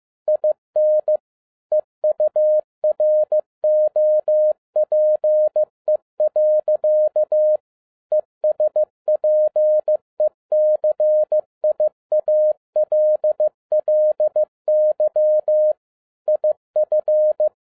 Morse code, heard as a keyed tone, about 15 wpm, 610 Hz. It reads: IN EUROPE. ESPECIALLY IF
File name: 2011-07-mp3 / n07 A LOT HAS CHANGED 15wpm